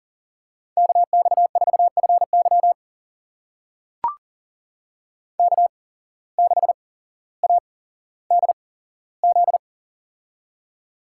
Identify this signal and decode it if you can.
Morse code, heard as a keyed tone, about 38 wpm, spaced as for 17 wpm, 700 Hz. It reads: KX4FY K6ADZ
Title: USA Call Signs - 8x Character Spacing - 40wpm